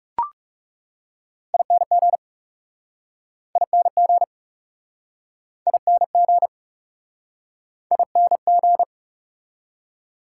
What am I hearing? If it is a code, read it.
Morse code, heard as a keyed tone, about 37 wpm, 700 Hz. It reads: ING ING ING ING
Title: Speed-Racing Prefixes and Suffixes 30wpm